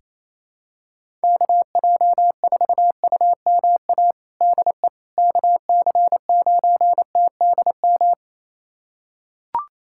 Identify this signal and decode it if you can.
Morse code, heard as a keyed tone, about 28 wpm, 700 Hz. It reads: KJ4UMA DE KC9TDM
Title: USA Callsign DE Callsign - 6 Characters - 28wpm